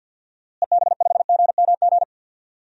Morse code, heard as a keyed tone, about 50 wpm, 700 Hz. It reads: EB5GMG
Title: DXCC-100 Call Signs - 6 Characters - Repeated 3x - 50wpm